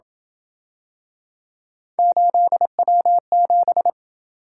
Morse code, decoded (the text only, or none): 8W7